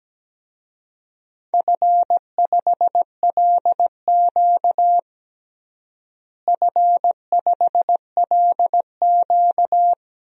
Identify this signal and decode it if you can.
Morse code, heard as a keyed tone, about 17 wpm, 700 Hz. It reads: F5LQ F5LQ
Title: DXCC-100 Call Signs - 4 Characters - Repeated 3x - 17wpm